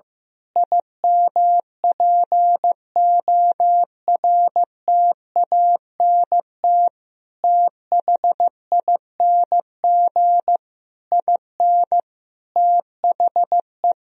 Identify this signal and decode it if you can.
Morse code, heard as a keyed tone, about 15 wpm, 700 Hz. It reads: IMPORTANT THING IN THE
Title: Sentences from Top 1000 Words - Review Part 3 - 15wpm